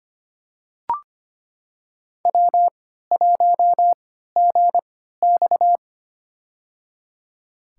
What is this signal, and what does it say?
Morse code, heard as a keyed tone, about 25 wpm, spaced as for 14 wpm, 700 Hz. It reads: W1GX
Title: USA Call Signs - 4 Characters - 3x Character Spacing - 25wpm